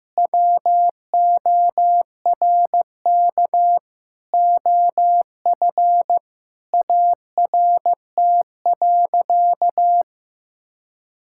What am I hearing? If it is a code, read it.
Morse code, heard as a keyed tone, about 16 wpm, 700 Hz. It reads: WORK OF ART.
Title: Sentences from Top 1000 Words - Review Part 6 - 15wpm